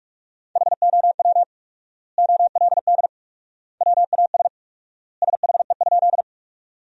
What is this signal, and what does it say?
Morse code, heard as a keyed tone, about 46 wpm, 700 Hz. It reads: HOW OLD WAS SHE?
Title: 4x Word-Spacing - Sentences from Top 200 Words - 45wpm